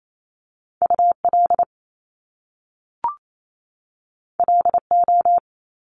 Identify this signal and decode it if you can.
Morse code, heard as a keyed tone, about 28 wpm, 700 Hz. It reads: UL LO
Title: Top Two Letter Combinations in Words 28wpm